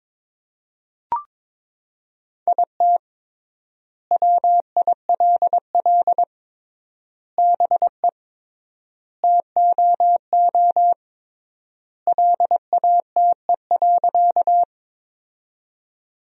Morse code, heard as a keyed tone, about 22 wpm, 700 Hz. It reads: IT WILL BE TOO LATE.
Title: 3x Word-Spacing - Sentences from Top 300 Words - 22wpm